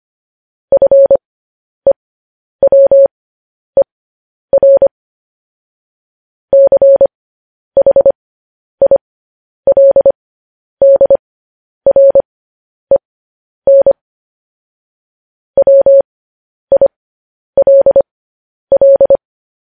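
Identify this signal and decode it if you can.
Morse code, heard as a keyed tone, about 25 wpm, spaced as for 10 wpm, 550 Hz. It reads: FEWER CHILDREN WILL
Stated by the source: Morse code news headlines practice transmission.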